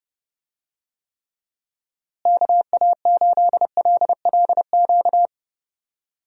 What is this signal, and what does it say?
Morse code, heard as a keyed tone, about 30 wpm, 700 Hz. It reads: KA8LLQ
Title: USA Call Signs - 6 Characters - Repeated 3x - 30wpm